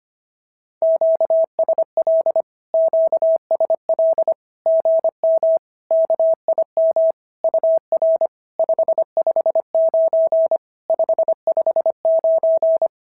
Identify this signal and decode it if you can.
Morse code, heard as a keyed tone, about 25 wpm, 650 Hz. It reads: QSL QSL GM KIM UR 559 559
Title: SOTA QSO Practice 25wpm